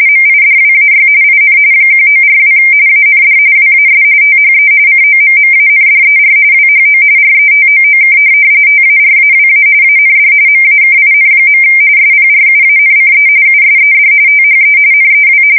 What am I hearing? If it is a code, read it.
RTTY, about 45.45 baud. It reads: MIDWAY IN 1942 AND WAS THE COMMANDER WHO RETOOK TARAWA AND MAKIN. SPRUANCE USED THE NEW
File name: fsk170 / USS-New-Jersey